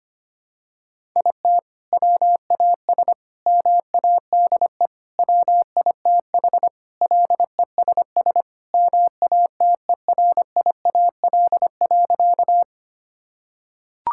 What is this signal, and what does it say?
Morse code, heard as a keyed tone, about 25 wpm, 700 Hz. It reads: IT WAS MADE WITH LESS MATERIAL.
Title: Sentences from Top 1000 Words - Review Part 5 - 25wpm